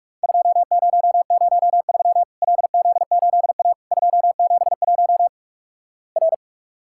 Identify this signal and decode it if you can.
Morse code, heard as a keyed tone, about 46 wpm, 700 Hz. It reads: 2003 LZ8A 271 R
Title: Worked All Europe DX Contest - QTC Exchange - 45wpm